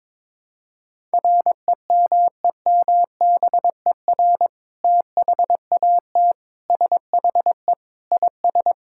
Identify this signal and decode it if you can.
Morse code, heard as a keyed tone, about 22 wpm, 700 Hz. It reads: REMEMBER THAT SHE IS
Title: Sentences from Top 1000 Words - Review Part 6 - 22wpm